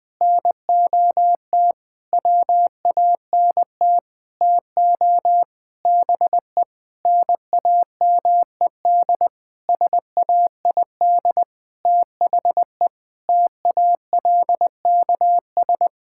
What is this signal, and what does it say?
Morse code, heard as a keyed tone, about 20 wpm, 700 Hz. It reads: NOT WANT TO BE NAMED SAID THE TALKS
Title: Sentences with 8-Word Phrases - 20wpm